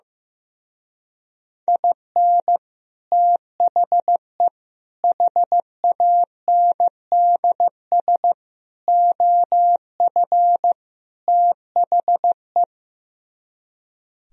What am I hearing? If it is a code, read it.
Morse code, heard as a keyed tone, about 16 wpm, 700 Hz. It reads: IN THE HANDS OF THE
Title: Speed-Racing - Sets of 5 Words 15wpm